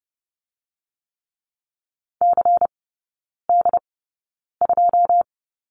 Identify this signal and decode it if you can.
Morse code, heard as a keyed tone, about 30 wpm, spaced as for 9 wpm, 700 Hz. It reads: CD2
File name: Lesson 17e - Letter u - Three Characters 30wpm